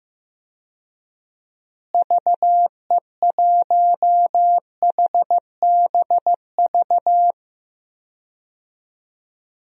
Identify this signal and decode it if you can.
Morse code, heard as a keyed tone, about 15 wpm, 700 Hz. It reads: VE1HBV
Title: DXCC-100 Call Signs - 6 Characters - Repeated 3x - 15wpm